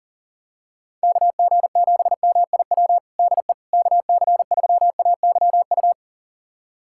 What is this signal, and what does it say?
Morse code, heard as a keyed tone, about 40 wpm, 700 Hz. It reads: KG7MIW DE KC3AYU
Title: USA Callsign DE Callsign 40wpm